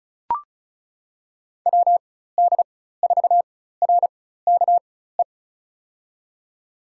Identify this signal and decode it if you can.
Morse code, heard as a keyed tone, about 36 wpm, spaced as for 16 wpm, 700 Hz. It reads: WD4RKE
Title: USA Call Signs - 6 Characters - 4x Character Spacing - 35wpm